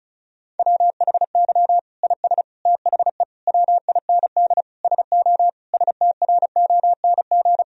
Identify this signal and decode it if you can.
Morse code, heard as a keyed tone, about 34 wpm, 700 Hz. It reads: WHY IS THE WIND SO STRONG
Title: Sentences from Top 1000 Words - Review Part 2 - 35wpm